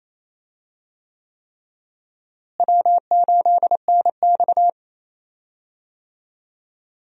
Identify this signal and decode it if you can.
Morse code, heard as a keyed tone, about 28 wpm, 700 Hz. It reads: W8NX
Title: USA Call Signs - 4 Characters - Repeated 3x - 28wpm